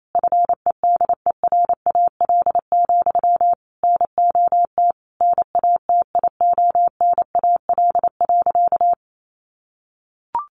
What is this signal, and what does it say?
Morse code, heard as a keyed tone, about 28 wpm, 700 Hz. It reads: FEDERAL, NOT NATIONAL.
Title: Sentences from Top 1200 Words 28wpm